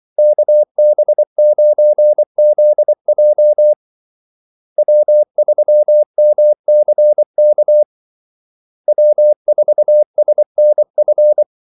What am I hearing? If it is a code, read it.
Morse code, heard as a keyed tone, about 24 wpm, 600 Hz. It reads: KB9ZJ W3MCK W4SNF
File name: Callsigns-24wpm